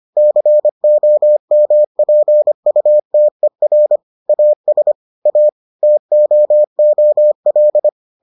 Morse code, heard as a keyed tone, about 25 wpm, 600 Hz. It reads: COMPUTER AS A TOOL